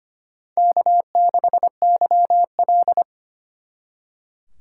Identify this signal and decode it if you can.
Morse code, heard as a keyed tone, about 32 wpm, 700 Hz. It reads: K6YL